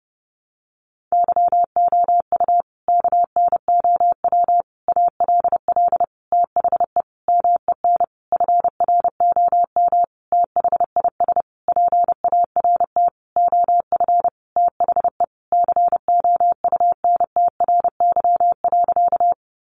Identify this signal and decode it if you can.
Morse code, heard as a keyed tone, about 30 wpm, 700 Hz. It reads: YOU KNOW ALL THE MEN FROM THIS PART OF THE COUNTRY.
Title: Sentences from Top 300 Words 30wpm